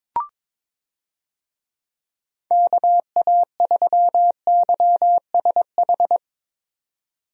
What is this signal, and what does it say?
Morse code, heard as a keyed tone, about 22 wpm, 700 Hz. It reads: KA3YSH
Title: USA Call Signs - 6 Characters - 22wpm